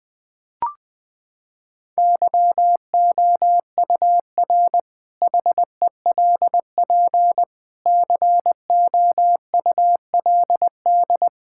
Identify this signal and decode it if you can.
Morse code, heard as a keyed tone, about 21 wpm, 700 Hz. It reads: YOUR HELP COULD
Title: Sentences with 8-Word Phrases - 20wpm